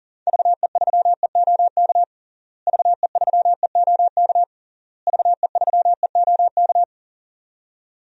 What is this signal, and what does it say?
Morse code, heard as a keyed tone, about 40 wpm, 700 Hz. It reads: VE3EOK VE3EOK VE3EOK
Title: DXCC-100 Call Signs - 6 Characters - Repeated 3x - 40wpm